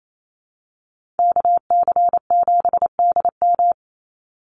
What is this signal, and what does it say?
Morse code, heard as a keyed tone, about 28 wpm, 700 Hz. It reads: KC7DM
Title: NA SOTA Activator Callsigns 28wpm